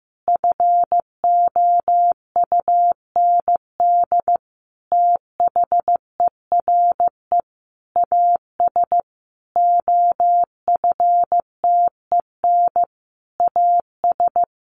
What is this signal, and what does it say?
Morse code, heard as a keyed tone, about 15 wpm, 700 Hz. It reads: FOUND THERE AS OFTEN AS
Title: Sentences from Top 1400 Words 15wpm